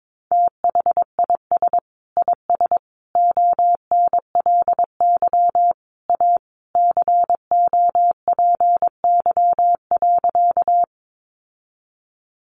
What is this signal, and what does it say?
Morse code, heard as a keyed tone, about 22 wpm, 700 Hz. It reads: THIS IS ONLY A COPY.
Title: Sentences from Top 1400 Words 22wpm